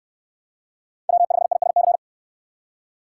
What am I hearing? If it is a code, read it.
Morse code, heard as a keyed tone, about 62 wpm, 700 Hz. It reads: W5ISC